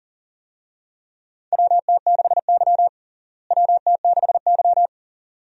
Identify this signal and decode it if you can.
Morse code, heard as a keyed tone, about 40 wpm, 700 Hz. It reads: WT6Y WT6Y
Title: USA Call Signs - 4 Characters - Repeated 2x - 40wpm